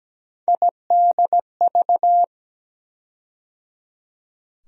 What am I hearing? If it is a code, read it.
Morse code, heard as a keyed tone, about 17 wpm, 700 Hz. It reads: IDV
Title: USA Call Sign Postfixes - 3 Letters 17wpm